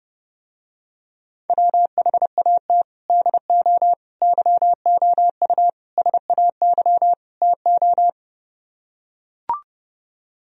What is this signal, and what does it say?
Morse code, heard as a keyed tone, about 30 wpm, 700 Hz. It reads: WHAT DO YOU SAY TO